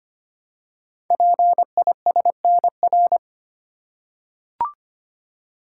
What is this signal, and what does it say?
Morse code, heard as a keyed tone, about 25 wpm, 700 Hz. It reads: PISNR
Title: Code Groups - 5 Letters - 25wpm